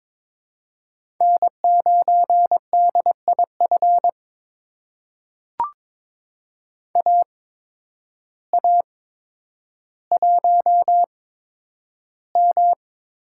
Morse code, heard as a keyed tone, about 22 wpm, spaced as for 10 wpm, 700 Hz. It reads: N9DIF AA1M
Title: USA Call Signs - 5 Characters - 8x Character Spacing - 22wpm